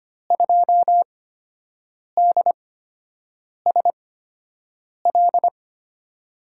Morse code, heard as a keyed tone, about 25 wpm, spaced as for 7 wpm, 700 Hz. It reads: 2DSL
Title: USA Call Signs - 8x Character Spacing - 25wpm